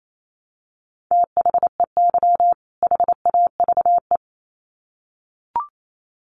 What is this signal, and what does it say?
Morse code, heard as a keyed tone, about 28 wpm, 700 Hz. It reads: THEY HAVE
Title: Sets of 2 Words 28wpm